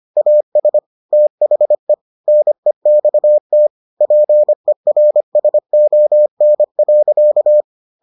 Morse code, heard as a keyed tone, about 25 wpm, 600 Hz. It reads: AS THE NEXT PERSON.